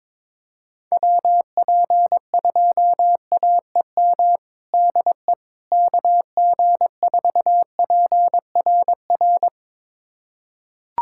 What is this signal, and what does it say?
Morse code, heard as a keyed tone, about 22 wpm, 700 Hz. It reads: WP2AEM DE KG4PRR
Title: USA Callsign DE Callsign - 6 Characters - 22wpm